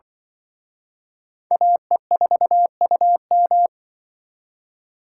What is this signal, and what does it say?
Morse code, heard as a keyed tone, about 25 wpm, 700 Hz. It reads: AE4UM
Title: Speed Racing - USA 2x2 Call Signs 20wpm